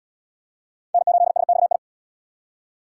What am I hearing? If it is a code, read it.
Morse code, heard as a keyed tone, about 77 wpm, 700 Hz. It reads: N8SXI